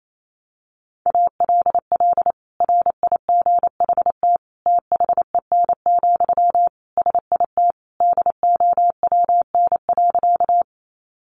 Morse code, heard as a keyed tone, about 28 wpm, 700 Hz. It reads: ALL RIGHT THEN, SIT DOWN.
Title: Sentences from Top 500 Words 28wpm